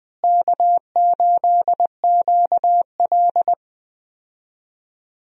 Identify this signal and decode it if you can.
Morse code, heard as a keyed tone, about 20 wpm, 700 Hz. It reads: K8QL
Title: USA Call Signs - 4 Characters - Repeated 3x - 20wpm